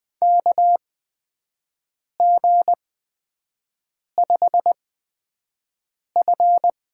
Morse code, heard as a keyed tone, about 20 wpm, spaced as for 5 wpm, 700 Hz. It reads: KG5F
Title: USA Call Signs - 8x Character Spacing - 20wpm